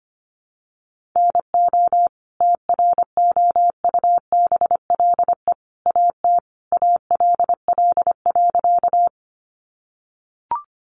Morse code, heard as a keyed tone, about 25 wpm, 700 Hz. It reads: NO TROUBLE AT ALL.
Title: Sentences from Top 1200 Words 25wpm